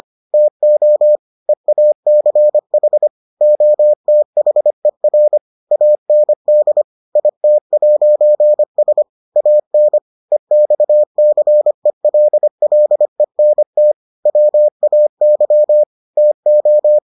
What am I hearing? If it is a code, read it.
Morse code, heard as a keyed tone, about 25 wpm, 600 Hz. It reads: TO EACH OTHER AND IT'S AN EXCELLENT WAY TO